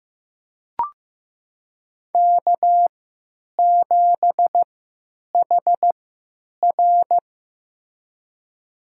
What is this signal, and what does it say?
Morse code, heard as a keyed tone, about 15 wpm, spaced as for 9 wpm, 700 Hz. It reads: K7HR
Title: USA Call Signs - 4 Characters - 3x Character Spacing - 15wpm